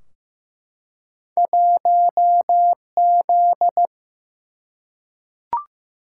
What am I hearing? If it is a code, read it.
Morse code, heard as a keyed tone, about 15 wpm, 700 Hz. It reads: 1Z
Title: Single-Letter Single-Number 15wpm